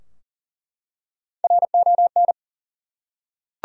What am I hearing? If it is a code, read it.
Morse code, heard as a keyed tone, about 40 wpm, 700 Hz. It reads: RON
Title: USA Common Names - Max 4 Letters - 40wpm